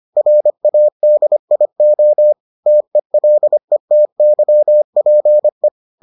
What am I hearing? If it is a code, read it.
Morse code, heard as a keyed tone, about 25 wpm, 600 Hz. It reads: RADIO TELETYPE